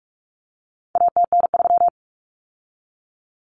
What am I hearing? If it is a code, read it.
Morse code, heard as a keyed tone, about 44 wpm, 700 Hz. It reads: ATN3
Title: Common Cut Numbers - 4-Digit 45wpm